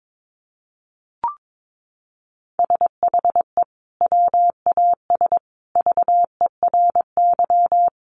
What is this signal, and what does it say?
Morse code, heard as a keyed tone, about 22 wpm, 700 Hz. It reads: SHE WAS VERY
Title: Sentences from Top 900 Words 22wpm